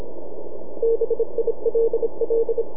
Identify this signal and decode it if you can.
Morse code, heard as a keyed tone, about 27 wpm, 460 Hz. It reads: BILL
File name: NAQPCW-8-2009 / W1UJ NAQP CW 09 1800Z